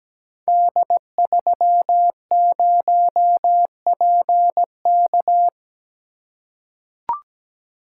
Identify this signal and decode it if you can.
Morse code, heard as a keyed tone, about 17 wpm, 700 Hz. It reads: D30PK
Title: Code Groups - 5 Letter:Numbers - 17wpm